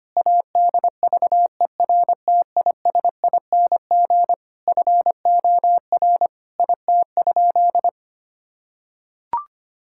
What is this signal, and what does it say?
Morse code, heard as a keyed tone, about 25 wpm, 700 Hz. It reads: ADVERTISING FOR IT?
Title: Sentences from Top 1000 Words - Review Part 6 - 25wpm